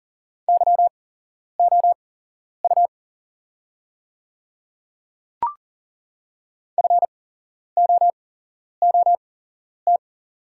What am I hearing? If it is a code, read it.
Morse code, heard as a keyed tone, about 39 wpm, spaced as for 11 wpm, 690 Hz. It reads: YOU FOOT
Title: 8x Character Spacing - Top 500 Words - 40wpm